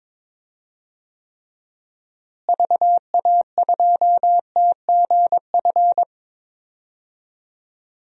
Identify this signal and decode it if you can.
Morse code, heard as a keyed tone, about 22 wpm, 700 Hz. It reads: VA2TGF
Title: DXCC-100 Call Signs - 6 Characters - Repeated 3x - 22wpm